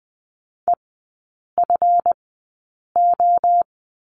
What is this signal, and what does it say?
Morse code, heard as a keyed tone, about 20 wpm, spaced as for 8 wpm, 700 Hz. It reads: EFO
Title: 159 Lesson 26e - Letter g - Three Characters 20wpm